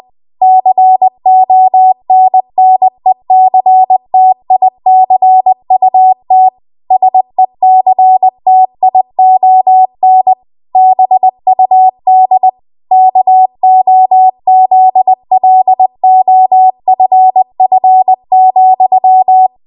Morse code, heard as a keyed tone, about 20 wpm, 750 Hz. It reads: CONNECTICUT SECTION BUD KOZLOFF,
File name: Morse / 20WPM